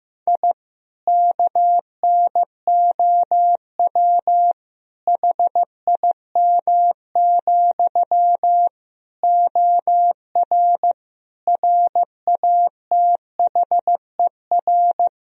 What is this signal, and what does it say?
Morse code, heard as a keyed tone, about 15 wpm, 700 Hz. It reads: I KNOW HIM, OR RATHER
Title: Sentences from Top 1000 Words - Review Part 4 - 15wpm